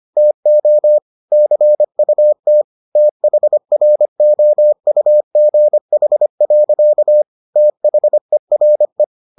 Morse code, heard as a keyed tone, about 25 wpm, 600 Hz. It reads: TO CUT THROUGH. THERE